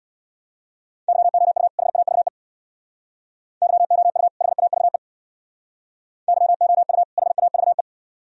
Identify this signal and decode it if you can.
Morse code, heard as a keyed tone, about 62 wpm, 700 Hz. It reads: YOU HAVE YOU HAVE YOU HAVE
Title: Speed-Racing - Sets of 2 Words 50wpm